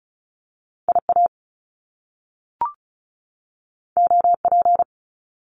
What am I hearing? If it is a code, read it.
Morse code, heard as a keyed tone, about 36 wpm, 700 Hz. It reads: IA OP
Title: Top Two Letter Combinations in Words 35wpm